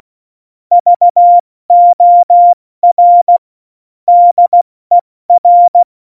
Morse code, heard as a keyed tone, about 16 wpm, spaced as for 14 wpm, 700 Hz. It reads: VOR DER
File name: Morsekurs / Witze / Witz 12 Mai, 2020 70 F80